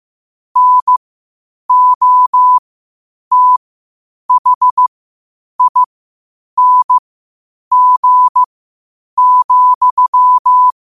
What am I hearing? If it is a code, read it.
Morse code, heard as a keyed tone, about 15 wpm, spaced as for 8 wpm, 1000 Hz. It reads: NOTHING,
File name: qotdfiles / 8wpm